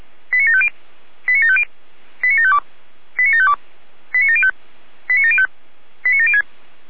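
Five-tone selective call (ZVEI), heard as quick runs of five tones, tones of about 70 ms. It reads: 89750 89750 89752 89752 89705 89705 89706
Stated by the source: Five-tone ZVEI selective call received on SDR.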